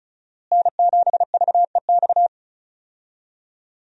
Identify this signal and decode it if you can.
Morse code, heard as a keyed tone, about 36 wpm, 700 Hz. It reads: N7VEX
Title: USA Call Signs - 5 Characters - Repeated 2x - 35wpm